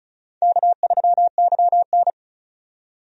Morse code, heard as a keyed tone, about 36 wpm, 700 Hz. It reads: K3YN
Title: USA Call Signs - 4 Characters - 35wpm